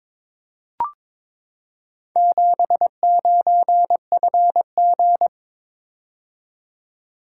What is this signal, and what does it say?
Morse code, heard as a keyed tone, about 22 wpm, 690 Hz. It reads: 79FG
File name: practice / 22wpm / Four Letter-Number Combinations - 22wpm